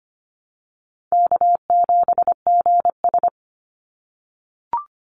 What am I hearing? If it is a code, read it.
Morse code, heard as a keyed tone, about 25 wpm, 700 Hz. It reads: K7GS
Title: Easy Call Signs - 25wpm